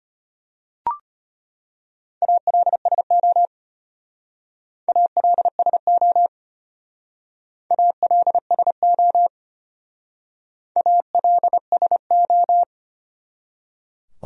Morse code, heard as a keyed tone, about 32 wpm, 700 Hz. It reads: ALSO ALSO ALSO ALSO